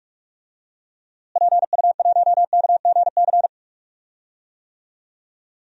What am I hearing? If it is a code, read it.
Morse code, heard as a keyed tone, about 46 wpm, 700 Hz. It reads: PU1KGC
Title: DXCC-100 Call Signs - 6 Characters - Repeated 3x - 45wpm